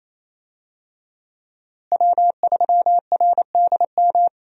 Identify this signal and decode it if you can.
Morse code, heard as a keyed tone, about 27 wpm, 700 Hz. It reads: W3RDM
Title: USA Call Signs - 5 Characters - Repeated 2x - 28wpm